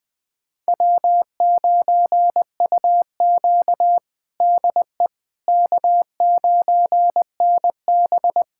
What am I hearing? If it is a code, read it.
Morse code, heard as a keyed tone, about 20 wpm, 700 Hz. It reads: W9UQ DE K9NB
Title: USA Callsign DE Callsign - 4 Characters - 20wpm